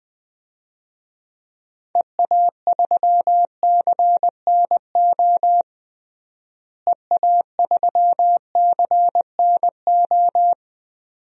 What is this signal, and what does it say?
Morse code, heard as a keyed tone, about 20 wpm, 700 Hz. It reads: EA3CNO EA3CNO
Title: DXCC-100 Call Signs - 6 Characters - Repeated 2x - 20wpm